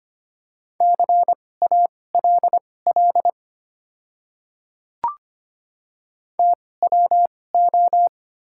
Morse code, heard as a keyed tone, about 25 wpm, spaced as for 18 wpm, 700 Hz. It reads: CALL TWO
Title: 2x Character Spacing - Top 100 Words - 25wpm